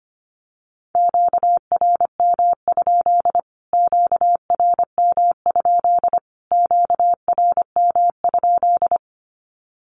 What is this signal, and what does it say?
Morse code, heard as a keyed tone, about 25 wpm, 700 Hz. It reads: QRM? QRM? QRM?
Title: Q-Codes Repeated 3x 25wpm